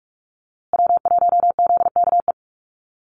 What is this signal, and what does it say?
Morse code, heard as a keyed tone, about 44 wpm, 690 Hz. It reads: W1ZKE
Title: USA Call Signs Repeated 3x 45wpm